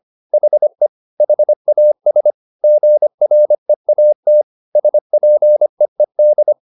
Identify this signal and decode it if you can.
Morse code, heard as a keyed tone, about 25 wpm, 600 Hz. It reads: HE HAS GREAT SPEED